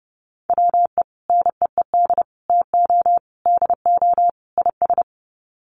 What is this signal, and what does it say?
Morse code, heard as a keyed tone, about 30 wpm, 700 Hz. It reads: WE NEED TO DO IS
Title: Sets of 5 Words Encore2 30wpm